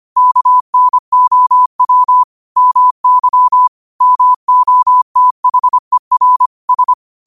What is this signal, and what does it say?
Morse code, heard as a keyed tone, about 25 wpm, 1000 Hz. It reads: KNOW MY MOTHER S
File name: qotdfiles / 25wpm